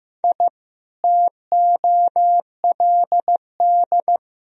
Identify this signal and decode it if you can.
Morse code, heard as a keyed tone, about 16 wpm, 700 Hz. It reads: I TOLD
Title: Sentences from Top 1000 Words - Review Part 1 - 15wpm